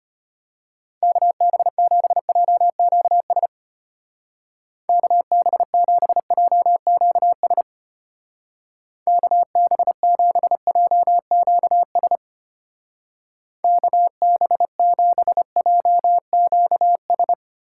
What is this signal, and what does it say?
Morse code, heard as a keyed tone, about 31 wpm, 700 Hz. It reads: KB7JQS KB7JQS KB7JQS KB7JQS
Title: Speed Racing - USA Call Signs 25wpm